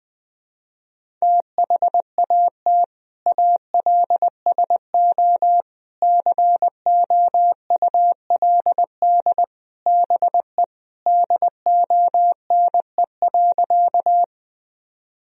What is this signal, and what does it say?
Morse code, heard as a keyed tone, about 21 wpm, 700 Hz. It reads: THAT ALSO COULD BE DONE.
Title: Sentences from Top 1000 Words - Review Part 1 - 20wpm